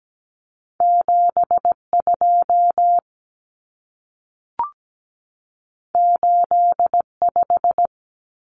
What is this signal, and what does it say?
Morse code, heard as a keyed tone, about 17 wpm, 700 Hz. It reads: 72 85
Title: Numbers - 2-Digit 17wpm